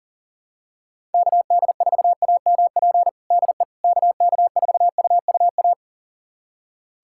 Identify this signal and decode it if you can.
Morse code, heard as a keyed tone, about 40 wpm, 700 Hz. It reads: KD4AMP DE KK4UUA
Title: USA Callsign DE Callsign 40wpm